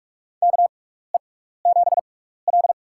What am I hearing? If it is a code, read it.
Morse code, heard as a keyed tone, about 42 wpm, spaced as for 15 wpm, 700 Hz. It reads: KE7L
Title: USA Call Signs - 4 Characters - 6x Character Spacing - 45wpm